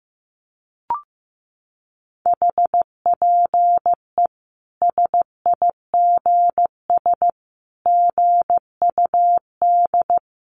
Morse code, heard as a keyed tone, about 15 wpm, 700 Hz. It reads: HPE SIGS GUD
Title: Common QSO Phrases 15wpm